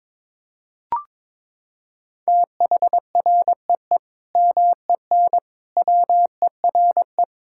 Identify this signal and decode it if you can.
Morse code, heard as a keyed tone, about 22 wpm, 700 Hz. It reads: THREE MEN WERE
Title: Sentences from Top 1600 Words 22wpm